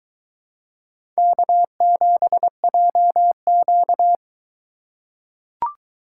Morse code, heard as a keyed tone, about 22 wpm, 700 Hz. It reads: K7JQ